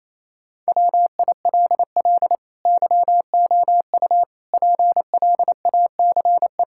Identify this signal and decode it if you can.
Morse code, heard as a keyed tone, about 28 wpm, 700 Hz. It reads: WILL YOU PLACE
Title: Sentences from Top 1600 Words 28wpm